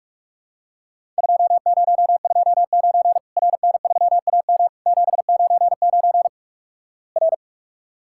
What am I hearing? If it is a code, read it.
Morse code, heard as a keyed tone, about 46 wpm, 700 Hz. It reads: 2029 RN3AM 799 R
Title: Worked All Europe DX Contest - QTC Exchange - 45wpm